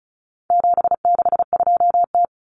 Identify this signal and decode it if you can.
Morse code, heard as a keyed tone, about 34 wpm, 700 Hz. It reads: 762T
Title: Common Cut Numbers - 4-Digit 35wpm